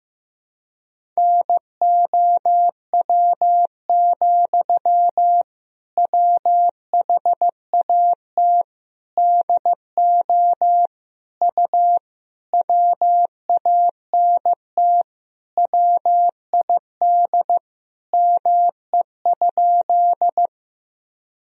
Morse code, heard as a keyed tone, about 16 wpm, 700 Hz. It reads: NOW, WHAT DO U WANT WID ME?
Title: Sentences from Top 200 Words and Abbreviations 15wpm